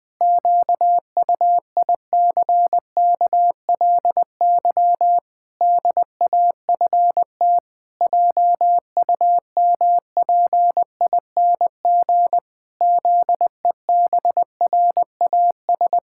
Morse code, heard as a keyed tone, about 21 wpm, 700 Hz. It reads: QUICKLY DAFT JUMPING ZEBRAS
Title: Pangram Challenge v2 - 20wpm